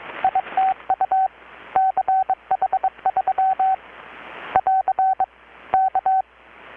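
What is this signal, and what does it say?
Morse code, heard as a keyed tone, about 23 wpm, 760 Hz, with radio noise. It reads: ITU CH3 <AR> K
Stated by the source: Morse from KPH's Night of Nights XX maritime radiotelegraphy event, received in Chiapas, Mexico about 3,800 km away on a Kenwood R-600 with a grounded 10 m long-wire antenna.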